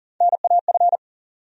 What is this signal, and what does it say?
Morse code, heard as a keyed tone, about 40 wpm, 700 Hz. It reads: NAF
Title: USA Call Sign Postfixes - 3 Letters 40wpm